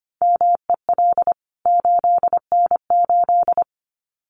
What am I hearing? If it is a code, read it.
Morse code, heard as a keyed tone, about 24 wpm, 700 Hz. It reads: MEL 8N8
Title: CWT Exchange 25wpm